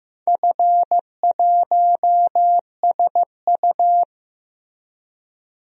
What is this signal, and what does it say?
Morse code, heard as a keyed tone, about 15 wpm, 700 Hz. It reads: F1SU
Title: DXCC-100 Call Signs - 4 Characters - Repeated 2x - 15wpm